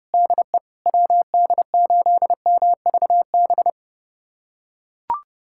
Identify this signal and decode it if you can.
Morse code, heard as a keyed tone, about 32 wpm, 700 Hz. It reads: DE WD8MVB
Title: USA Callsign DE Callsign 30wpm